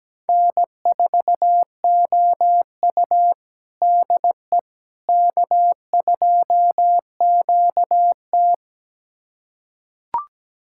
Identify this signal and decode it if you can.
Morse code, heard as a keyed tone, about 17 wpm, 700 Hz. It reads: N4OU DE K2QT
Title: USA Callsign DE Callsign - 4 Characters - 17wpm